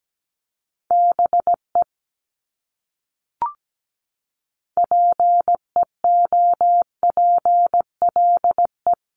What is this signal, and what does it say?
Morse code, heard as a keyed tone, about 17 wpm, 700 Hz. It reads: BE PEOPLE
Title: Top 300 Words Repeated 17wpm